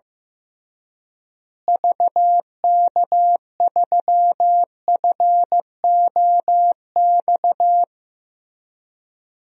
Morse code, heard as a keyed tone, about 16 wpm, 700 Hz. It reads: VK3FOX
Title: DXCC-100 Call Signs - 6 Characters - Repeated 3x - 15wpm